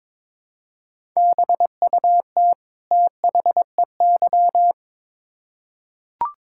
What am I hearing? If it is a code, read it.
Morse code, heard as a keyed tone, about 22 wpm, 700 Hz. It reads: BUT THEY
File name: Build A Phrase - 2-Word Phrases - 22wpm